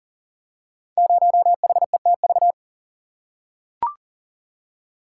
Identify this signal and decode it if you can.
Morse code, heard as a keyed tone, about 40 wpm, 690 Hz. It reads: 0HETV
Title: Code Groups - 5 Letter:Numbers - 40wpm